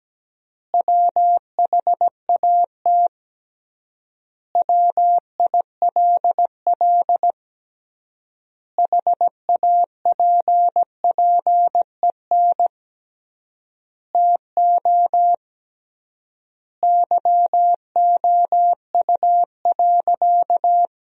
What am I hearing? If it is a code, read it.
Morse code, heard as a keyed tone, about 17 wpm, 700 Hz. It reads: WHAT WILL HAPPEN TO YOU.
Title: 3x Word-Spacing - Sentences from Top 500 Words - 17wpm